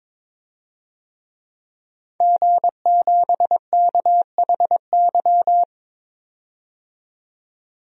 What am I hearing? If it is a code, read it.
Morse code, heard as a keyed tone, about 22 wpm, 700 Hz. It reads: G7KHY
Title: DXCC-100 Call Signs - 5 Characters - 22wpm